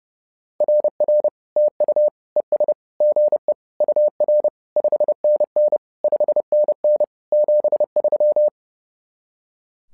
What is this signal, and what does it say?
Morse code, heard as a keyed tone, about 30 wpm, 600 Hz. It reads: RR TU ES GE UR 5NN 5NN 73
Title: SOTA QSO Practice 30wpm